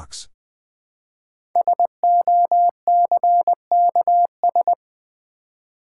Morse code, heard as a keyed tone, about 20 wpm, 700 Hz. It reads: SOCKS
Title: Easy Words with Hints - 20wpm